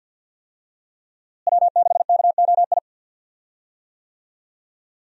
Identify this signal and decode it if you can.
Morse code, heard as a keyed tone, about 47 wpm, 700 Hz. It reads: W6KOI
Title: USA Call Signs - 5 Characters - Repeated 3x - 50wpm